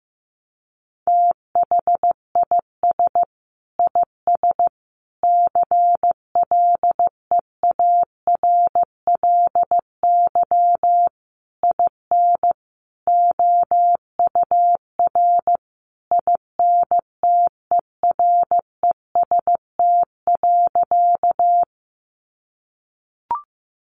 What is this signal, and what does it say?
Morse code, heard as a keyed tone, about 15 wpm, 700 Hz. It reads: THIS IS CLEARLY IN OUR INTEREST.
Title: Sentences from Top 1300 Words 15wpm